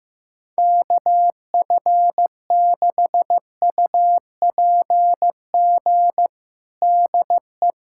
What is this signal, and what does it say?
Morse code, heard as a keyed tone, about 16 wpm, 700 Hz. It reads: KF6UPG DE
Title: USA Callsign DE Callsign - 6 Characters - 15wpm